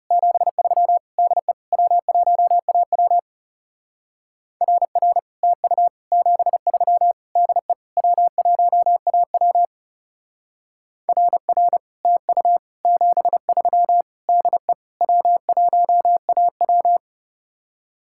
Morse code, heard as a keyed tone, about 36 wpm, 700 Hz. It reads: 73 DE W1AW RR TU 73 DE W1AW RR TU 73 DE W1AW
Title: Speed-Racing Common QSO Phrases - 30wpm